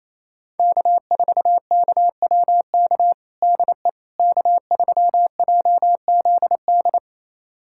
Morse code, heard as a keyed tone, about 28 wpm, 700 Hz. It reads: K4KWK DE K3JZD
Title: USA Callsign DE Callsign - 5 Characters - 28wpm